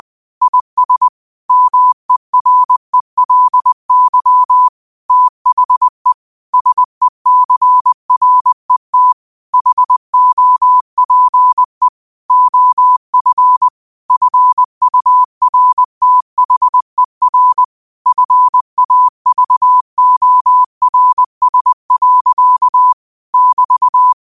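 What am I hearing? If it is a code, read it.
Morse code, heard as a keyed tone, about 20 wpm, 1000 Hz. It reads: IS MERELY THE SECRET HOPE OF FURTHER FAVORS. <BT>
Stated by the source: Morse code practice at 20 wpm: famous quotations sent in telegraphy, from URVAG.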